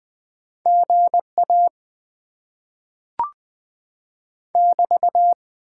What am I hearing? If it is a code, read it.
Morse code, heard as a keyed tone, about 20 wpm, 700 Hz. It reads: GA <BT>
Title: Top 90 QSO Elements - 8x Character Spacing - 20wpm